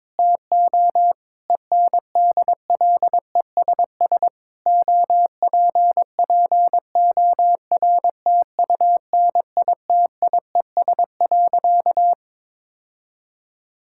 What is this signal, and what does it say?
Morse code, heard as a keyed tone, about 22 wpm, 700 Hz. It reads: TO ENDLESS OPPORTUNITIES.
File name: Difficult Sentences with Hints - 22wpm